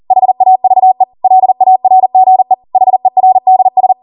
Morse code, heard as a keyed tone, about 40 wpm, 750 Hz. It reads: HAVE LARGE HERDS